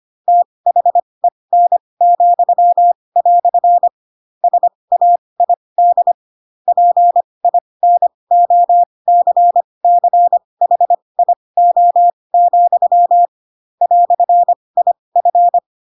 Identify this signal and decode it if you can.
Morse code, heard as a keyed tone, about 25 wpm, spaced as for 20 wpm, 700 Hz. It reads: THEN," SAID PINOCCHIO, "IF
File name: Pinocchio / Pinocchio ch7 15 25 ik1hge